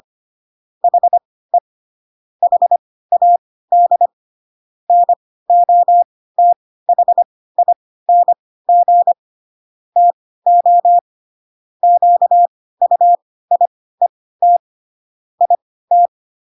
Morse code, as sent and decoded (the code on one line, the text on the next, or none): .... / .... .- -.. / -. --- - .... .. -. --. / - --- / --.- ..- .. - / .. -
HE HAD NOTHING TO QUIET IT